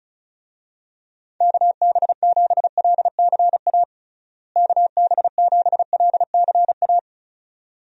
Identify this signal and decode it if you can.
Morse code, heard as a keyed tone, about 36 wpm, 700 Hz. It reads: KB7LCA KB7LCA
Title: USA Call Signs - 6 Characters - Repeated 2x - 35wpm